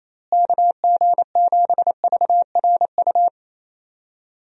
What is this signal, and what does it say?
Morse code, heard as a keyed tone, about 28 wpm, 700 Hz. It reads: KG7VRU
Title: USA Call Signs - 6 Characters - Repeated 3x - 28wpm